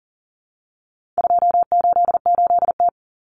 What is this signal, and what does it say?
Morse code, heard as a keyed tone, about 40 wpm, 700 Hz. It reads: 288T
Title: Common Cut Numbers - 4-Digit 40wpm